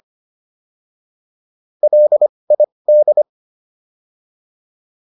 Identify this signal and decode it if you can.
Morse code, heard as a keyed tone, about 25 wpm, spaced as for 20 wpm, 600 Hz. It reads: LID